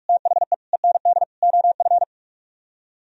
Morse code, heard as a keyed tone, about 44 wpm, 700 Hz. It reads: THE END OF